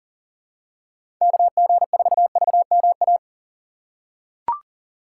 Morse code, heard as a keyed tone, about 40 wpm, 700 Hz. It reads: KG4VMA
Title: USA Call Signs - 6 Characters - 2x Character Spacing - 40wpm